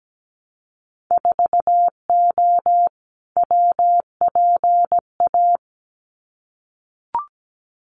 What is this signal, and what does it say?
Morse code, heard as a keyed tone, about 17 wpm, 700 Hz. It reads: 4O WPA
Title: Winter Field Day Exchanges 17wpm